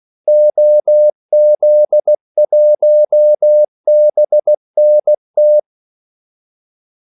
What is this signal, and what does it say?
Morse code, heard as a keyed tone, about 16 wpm, 600 Hz. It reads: OZ1BNT
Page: Kaldesignaler 16 wpm